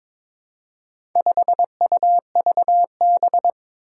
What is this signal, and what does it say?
Morse code, heard as a keyed tone, about 22 wpm, 700 Hz. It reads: 5UVB
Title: Four Challenging Letter-Number Combinations 22wpm